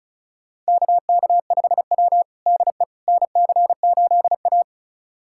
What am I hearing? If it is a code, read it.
Morse code, heard as a keyed tone, about 36 wpm, 700 Hz. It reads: KK5W DE NC8A
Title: USA Callsign DE Callsign - 4 Characters - 35wpm